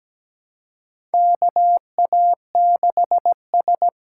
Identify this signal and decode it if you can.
Morse code, heard as a keyed tone, about 17 wpm, 700 Hz. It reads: KA6S
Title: USA Call Signs - 4 Characters - 4x Character Spacing - 17wpm